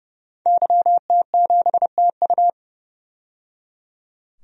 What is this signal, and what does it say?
Morse code, heard as a keyed tone, about 29 wpm, 700 Hz. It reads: YT7TU
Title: DXCC-100 Call Signs - 5 Characters - 30wpm